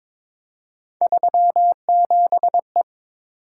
Morse code, heard as a keyed tone, about 22 wpm, 700 Hz. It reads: 37E
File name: practice / 22wpm / Three Letter-Number Combinations - 22wpm